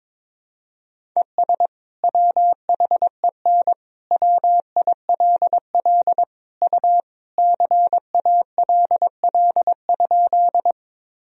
Morse code, heard as a keyed tone, about 22 wpm, 700 Hz. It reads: ES WHEN WILL U CALL?
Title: Sentences from Top 100 Words and Abbreviations 22wpm